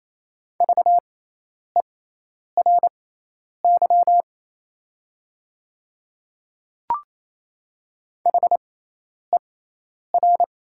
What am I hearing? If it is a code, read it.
Morse code, heard as a keyed tone, about 28 wpm, spaced as for 10 wpm, 690 Hz. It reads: VERY HER
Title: 6x Character Spacing - Top 200 Words - 28wpm